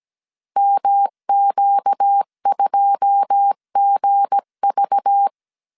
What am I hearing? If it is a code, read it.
Morse code, heard as a keyed tone, about 17 wpm, 800 Hz. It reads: MQ2GV
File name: CW / DET / 1 9 1500C S B 16wpm 2856